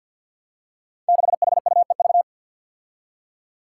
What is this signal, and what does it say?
Morse code, heard as a keyed tone, about 51 wpm, 700 Hz. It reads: 6HUEV
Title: Five Challenging Letter-Number Combinations 50wpm